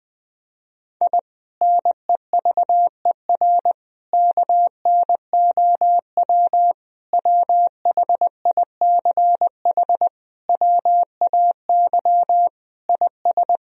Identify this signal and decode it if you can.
Morse code, heard as a keyed tone, about 20 wpm, 700 Hz. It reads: I NEVER KNOW WHICH WAY IS
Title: Sentences from Top 1000 Words - Review Part 3 - 20wpm